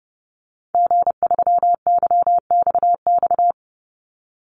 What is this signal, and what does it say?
Morse code, heard as a keyed tone, about 30 wpm, 700 Hz. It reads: G3YXX
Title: Call Signs - 30wpm